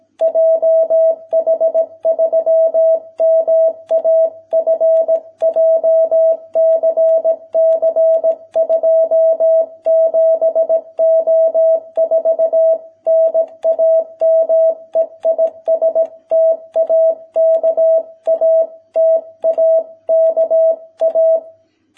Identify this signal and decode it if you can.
Morse code, heard as a keyed tone, about 18 wpm, 650 Hz. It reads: JH3MAF JCC 27O4 NAME IS TAKA TAKA